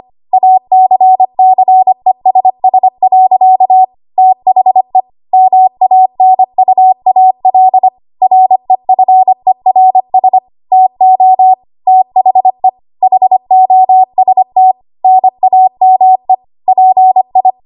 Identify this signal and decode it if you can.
Morse code, heard as a keyed tone, about 25 wpm, 750 Hz. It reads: ACCESS. THE MANUAL REFERS TO THE HOST NAME PI
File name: Archive / 25 WPM / 25WPM